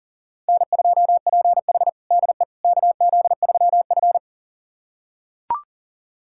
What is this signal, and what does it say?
Morse code, heard as a keyed tone, about 40 wpm, 700 Hz. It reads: N2PH DE KZ3F
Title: USA Callsign DE Callsign - 4 Characters - 40wpm